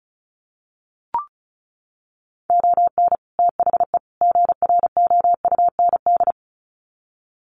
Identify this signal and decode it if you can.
Morse code, heard as a keyed tone, about 34 wpm, 700 Hz. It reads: ON THE GROUND
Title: Sets of 3 Words 35wpm